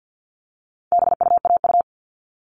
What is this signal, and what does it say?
Morse code, heard as a keyed tone, about 50 wpm, 700 Hz. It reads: BUAU
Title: All Cut Numbers - 4-Digit 50wpm